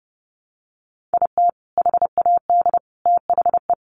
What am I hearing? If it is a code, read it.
Morse code, heard as a keyed tone, about 30 wpm, 700 Hz. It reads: IT HAD THE
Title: Sentences from Top 1300 Words 30wpm